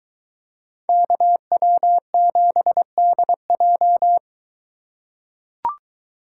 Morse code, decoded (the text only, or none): KW7DJ